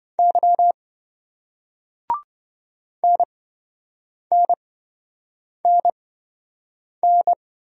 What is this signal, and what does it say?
Morse code, heard as a keyed tone, about 26 wpm, spaced as for 7 wpm, 700 Hz. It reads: Y NNNN